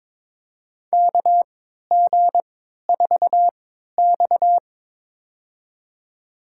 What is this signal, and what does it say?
Morse code, heard as a keyed tone, about 22 wpm, spaced as for 13 wpm, 700 Hz. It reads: KG4X
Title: USA Call Signs - 4 Characters - 3x Character Spacing - 22wpm